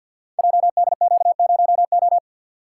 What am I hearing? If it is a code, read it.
Morse code, heard as a keyed tone, about 50 wpm, 700 Hz. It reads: JDQ0O
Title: Code Groups - 5 Letter:Numbers - 50wpm